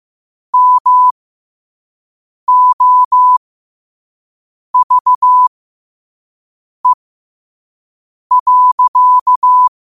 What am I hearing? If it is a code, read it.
Morse code, heard as a keyed tone, about 15 wpm, spaced as for 5 wpm, 1000 Hz. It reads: MOVE.
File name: 5wpm